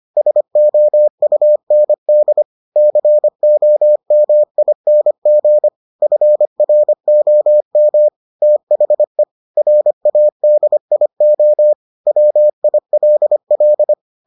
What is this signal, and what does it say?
Morse code, heard as a keyed tone, about 25 wpm, 600 Hz. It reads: SOUND COMING FROM THE RADIO WILL